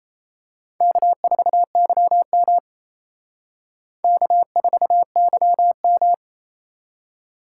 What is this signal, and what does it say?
Morse code, heard as a keyed tone, about 32 wpm, 700 Hz. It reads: K4YM K4YM
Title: Speed Racing - USA 1x2 Call Signs 28wpm